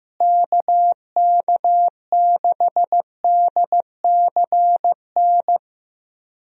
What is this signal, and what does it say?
Morse code, heard as a keyed tone, about 15 wpm, 700 Hz. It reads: KK6DCN
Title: USA Call Signs - 6 Characters - Repeated 3x - 15wpm